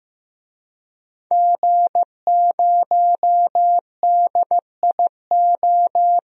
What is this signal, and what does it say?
Morse code, heard as a keyed tone, about 15 wpm, 700 Hz. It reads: G0DIO
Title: DXCC-100 Call Signs - 5 Characters - 15wpm